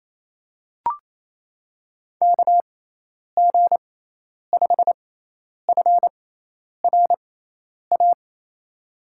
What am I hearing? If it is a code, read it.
Morse code, heard as a keyed tone, about 27 wpm, spaced as for 10 wpm, 700 Hz. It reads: KG5FRA
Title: USA Call Signs - 6 Characters - 6x Character Spacing - 28wpm